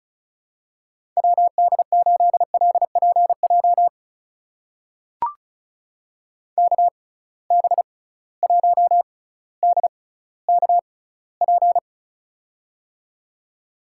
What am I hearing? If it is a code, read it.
Morse code, heard as a keyed tone, about 36 wpm, spaced as for 18 wpm, 700 Hz. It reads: WD8LPJ KB1DKP
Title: USA Call Signs - 6 Characters - 6x Character Spacing - 35wpm